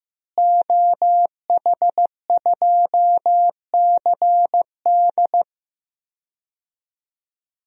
Morse code, heard as a keyed tone, about 16 wpm, 700 Hz. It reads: OH2CD
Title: DXCC-100 Call Signs - 5 Characters - Repeated 2x - 15wpm